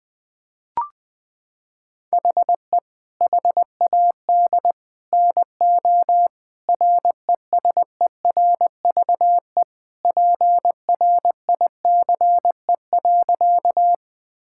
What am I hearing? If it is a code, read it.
Morse code, heard as a keyed tone, about 21 wpm, 700 Hz. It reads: HE HAD NO RESERVE PRICE.
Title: Sentences from Top 1000 Words - Review Part 6 - 20wpm